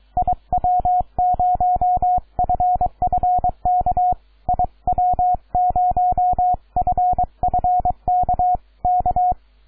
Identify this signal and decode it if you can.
Morse code, heard as a keyed tone, about 23 wpm, 720 Hz, with radio noise. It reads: IW0FFK IW0FFK K
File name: iw0ffk / 2014 03 09 CX9AU IW0FFK